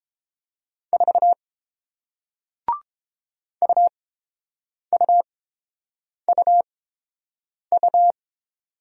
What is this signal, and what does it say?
Morse code, heard as a keyed tone, about 30 wpm, spaced as for 7 wpm, 700 Hz. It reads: 4 UUUU